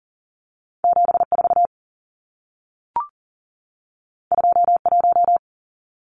Morse code, heard as a keyed tone, about 40 wpm, 700 Hz. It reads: 74 21
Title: Numbers - 2-Digit 40wpm